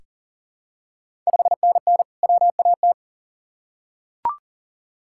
Morse code, heard as a keyed tone, about 40 wpm, 690 Hz. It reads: 5NN WAT
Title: Ontario QSO Party Exchange 40wpm